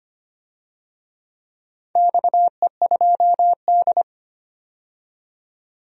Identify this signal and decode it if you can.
Morse code, heard as a keyed tone, about 25 wpm, 700 Hz. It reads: XE2D
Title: DXCC-100 Call Signs - 4 Characters - 25wpm